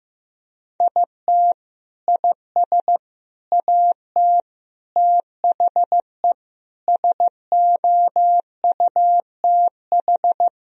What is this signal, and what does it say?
Morse code, heard as a keyed tone, about 16 wpm, 700 Hz. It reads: IT IS AT THE SOUTH
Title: Sentences from Top 1000 Words - Review Part 2 - 15wpm